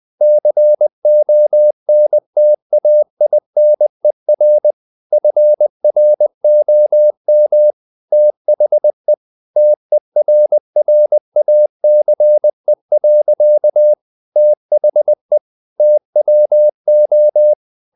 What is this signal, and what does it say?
Morse code, heard as a keyed tone, about 20 wpm, 600 Hz. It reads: CONTAINER FROM THE TERRACE. THE TWO
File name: the old man and the sea / the old man and the sea part 1 20wpm